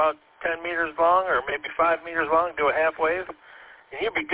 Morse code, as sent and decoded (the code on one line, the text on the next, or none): none